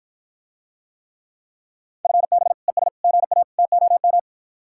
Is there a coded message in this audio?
Yes: Morse code from 2.0 to 4.2 s, heard as a keyed tone, about 54 wpm, 700 Hz. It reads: FB ES GA TOM